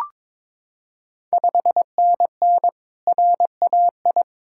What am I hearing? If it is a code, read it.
Morse code, heard as a keyed tone, about 22 wpm, 700 Hz. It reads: 5NN RAI
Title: Ontario QSO Party Exchange 22wpm